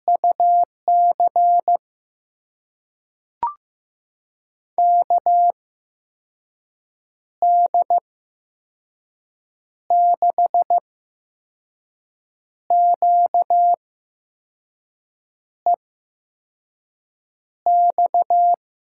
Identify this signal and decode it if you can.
Morse code, heard as a keyed tone, about 15 wpm, spaced as for 5 wpm, 700 Hz. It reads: UC KD6QEX
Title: USA Call Signs - 6 Characters - 8x Character Spacing - 15wpm